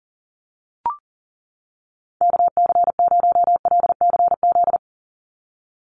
Morse code, heard as a keyed tone, about 40 wpm, 700 Hz. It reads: KC0LCZ
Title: USA Call Signs 40wpm